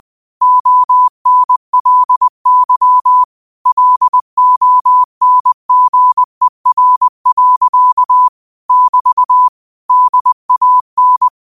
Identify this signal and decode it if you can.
Morse code, heard as a keyed tone, about 20 wpm, 1000 Hz. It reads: ONLY LONGER. <BT> DAN
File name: qotdfiles / 20wpm